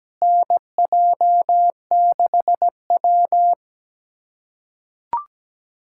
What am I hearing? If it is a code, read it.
Morse code, heard as a keyed tone, about 17 wpm, 700 Hz. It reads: NJ6W
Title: USA Callsign DE Callsign 17wpm